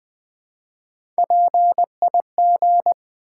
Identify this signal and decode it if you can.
Morse code, heard as a keyed tone, about 20 wpm, 700 Hz. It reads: PIG
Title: Build A Word - 1 Letter At A Time - 3-Letter Words - 20wpm